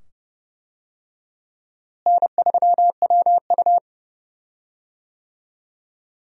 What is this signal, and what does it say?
Morse code, heard as a keyed tone, about 29 wpm, 700 Hz. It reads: N3WU
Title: USA Call Signs - 4 Characters - Repeated 2x - 30wpm